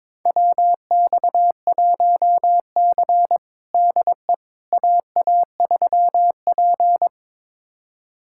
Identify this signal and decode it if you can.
Morse code, heard as a keyed tone, about 22 wpm, 700 Hz. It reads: WX1C DE AA3P
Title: USA Callsign DE Callsign - 4 Characters - 22wpm